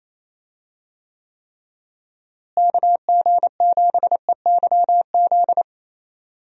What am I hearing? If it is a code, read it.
Morse code, heard as a keyed tone, about 27 wpm, 700 Hz. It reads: KG7EYZ
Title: USA Call Signs - 6 Characters - Repeated 3x - 28wpm